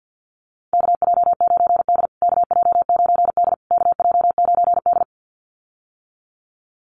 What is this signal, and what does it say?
Morse code, heard as a keyed tone, about 52 wpm, 700 Hz. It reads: KJ9D KJ9D KJ9D
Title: USA Call Signs Repeated 3x 50wpm